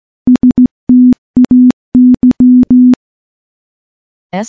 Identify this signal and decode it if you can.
Morse code, heard as a keyed tone, about 15 wpm, 250 Hz. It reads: STAY
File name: morse-training-15WPM